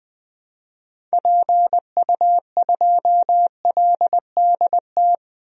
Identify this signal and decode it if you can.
Morse code, heard as a keyed tone, about 20 wpm, 700 Hz. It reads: PU2LDT
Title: DXCC-100 Call Signs - 6 Characters - 20wpm